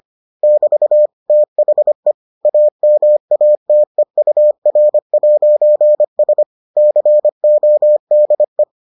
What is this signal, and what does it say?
Morse code, heard as a keyed tone, about 25 wpm, 600 Hz. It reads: <BT> THE AMATEUR'S CODE